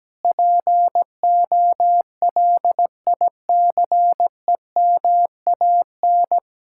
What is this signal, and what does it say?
Morse code, heard as a keyed tone, about 17 wpm, 700 Hz. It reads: POLICEMAN
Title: Speed-Racing Common QSO Phrases - 17wpm